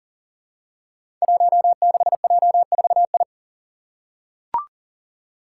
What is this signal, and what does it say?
Morse code, heard as a keyed tone, about 38 wpm, 700 Hz. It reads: 16J4I
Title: Five Challenging Letter-Number Combinations 40wpm